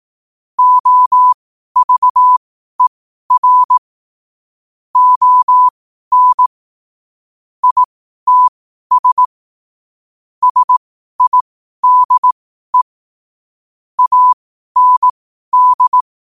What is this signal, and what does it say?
Morse code, heard as a keyed tone, about 18 wpm, spaced as for 12 wpm, 1000 Hz. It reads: OVER ON ITS SIDE AND
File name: qotdfiles / 12wpm